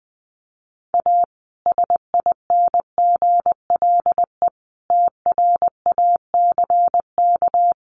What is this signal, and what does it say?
Morse code, heard as a keyed tone, about 19 wpm, 700 Hz. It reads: A SINGLE TRACK
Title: Sentences from Top 1100 Words 20wpm